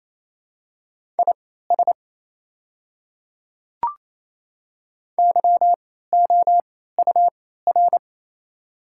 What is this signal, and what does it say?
Morse code, heard as a keyed tone, about 28 wpm, spaced as for 16 wpm, 700 Hz. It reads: IS YOUR
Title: 3x Character Spacing - Top 200 Words - 28wpm